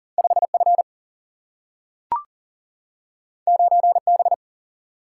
Morse code, heard as a keyed tone, about 40 wpm, 700 Hz. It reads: 5 F 9 B
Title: Single-Letter Single-Number 40wpm